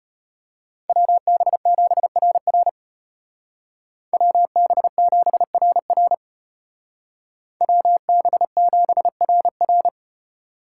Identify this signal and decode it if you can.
Morse code, heard as a keyed tone, about 30 wpm, 700 Hz. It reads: WB7RR WB7RR WB7RR
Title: Speed Racing - USA 2x2 Call Signs 25wpm